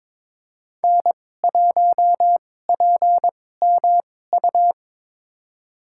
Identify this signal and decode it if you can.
Morse code, heard as a keyed tone, about 22 wpm, spaced as for 16 wpm, 700 Hz. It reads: N1PMU
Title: USA Call Signs - 2x Character Spacing - 22wpm